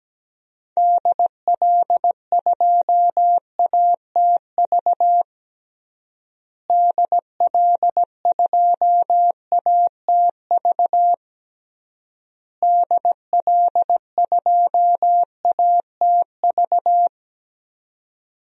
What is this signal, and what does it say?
Morse code, heard as a keyed tone, about 17 wpm, 700 Hz. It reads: DL2ATV DL2ATV DL2ATV
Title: DXCC-100 Call Signs - 6 Characters - Repeated 3x - 17wpm